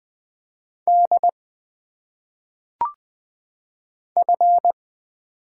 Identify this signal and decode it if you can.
Morse code, heard as a keyed tone, about 20 wpm, 700 Hz. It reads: D F
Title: Single Letters - Phonetic Pronunciation 20wpm